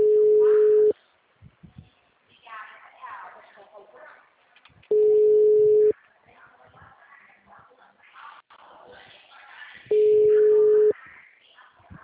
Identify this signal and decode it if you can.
Ringback tone (Europe, 425 Hz), with speech.
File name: out- -2050- - - .34092